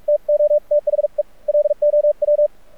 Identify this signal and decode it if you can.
Morse code, heard as a keyed tone, about 43 wpm, 590 Hz. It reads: TO THE LOW
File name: archived / CRSnet